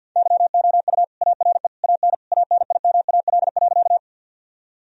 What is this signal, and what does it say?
Morse code, heard as a keyed tone, about 52 wpm, 700 Hz. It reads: YOU ARE AN ANIMAL.
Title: Sentences from Top 1000 Words - Review Part 1 - 50wpm